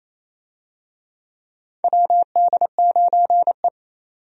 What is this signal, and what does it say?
Morse code, heard as a keyed tone, about 28 wpm, 700 Hz. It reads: WD9E
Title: USA Call Signs - 4 Characters - 28wpm